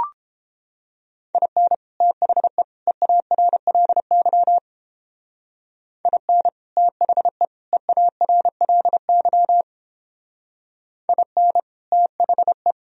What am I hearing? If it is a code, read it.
Morse code, heard as a keyed tone, about 29 wpm, 700 Hz. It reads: IN THE EARLY IN THE EARLY IN THE